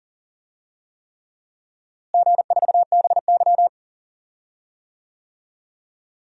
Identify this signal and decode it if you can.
Morse code, heard as a keyed tone, about 40 wpm, 700 Hz. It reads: G4BY
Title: DXCC-100 Call Signs - 4 Characters - Repeated 3x - 40wpm